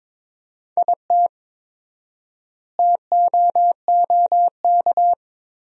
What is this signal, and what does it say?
Morse code, heard as a keyed tone, about 22 wpm, 700 Hz. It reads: IT TOOK
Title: 4x Word-Spacing - Sentences from Top 300 Words - 22wpm